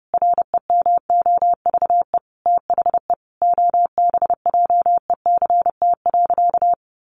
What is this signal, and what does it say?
Morse code, heard as a keyed tone, about 30 wpm, 700 Hz. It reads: REMOVE THE OBJECT.
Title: Sentences from Top 800 Words 30wpm